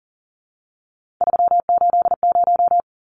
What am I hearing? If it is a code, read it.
Morse code, heard as a keyed tone, about 40 wpm, 700 Hz. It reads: 380
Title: Numbers - 3-Digit 40wpm